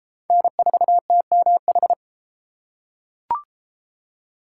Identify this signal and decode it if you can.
Morse code, heard as a keyed tone, about 30 wpm, 700 Hz. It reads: N4TMH